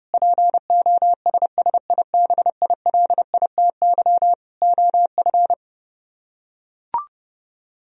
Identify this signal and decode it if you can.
Morse code, heard as a keyed tone, about 30 wpm, 700 Hz. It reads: POSSIBILITY OF